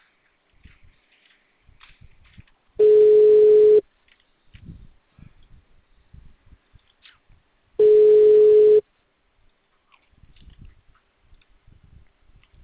Ringback tone (Europe, 425 Hz).
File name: out- -2012- - - .1207